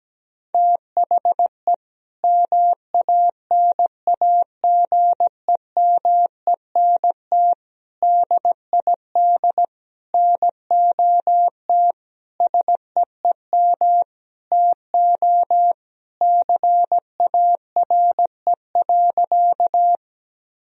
Morse code, heard as a keyed tone, about 17 wpm, 700 Hz. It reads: THE MANAGEMENT DID NOT SEEM TO CARE.
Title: Sentences from Top 1000 Words - Review Part 4 - 17wpm